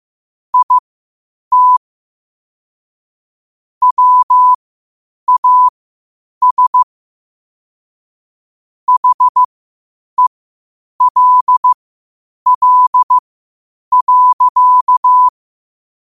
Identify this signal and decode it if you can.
Morse code, heard as a keyed tone, about 15 wpm, spaced as for 8 wpm, 1000 Hz. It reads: IT WAS HELL.
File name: qotdfiles / 8wpm